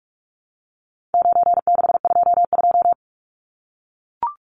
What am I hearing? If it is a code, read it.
Morse code, heard as a keyed tone, about 44 wpm, 700 Hz. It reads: 9622
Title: Numbers - 4-Digit 45wpm